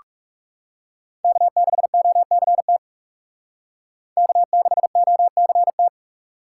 Morse code, heard as a keyed tone, about 38 wpm, 700 Hz. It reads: K6OCT K6OCT